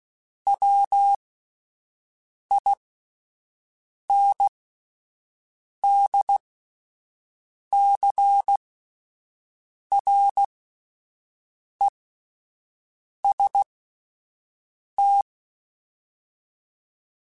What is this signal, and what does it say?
Morse code, heard as a keyed tone, about 16 wpm, spaced as for 5 wpm, 800 Hz. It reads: WINDCREST